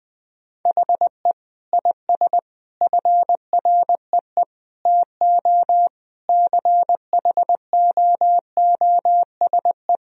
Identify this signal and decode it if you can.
Morse code, heard as a keyed tone, about 20 wpm, 700 Hz. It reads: HE IS FREE TO CHOOSE
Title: Sentences from Top 1000 Words - Review Part 4 - 20wpm